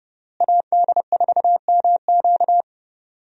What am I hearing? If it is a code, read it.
Morse code, heard as a keyed tone, about 30 wpm, 700 Hz. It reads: AD4MQ
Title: USA Call Signs - 5 Characters - Repeated 3x - 30wpm